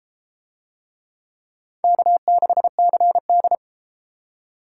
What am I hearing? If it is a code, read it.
Morse code, heard as a keyed tone, about 31 wpm, 700 Hz. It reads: K6CD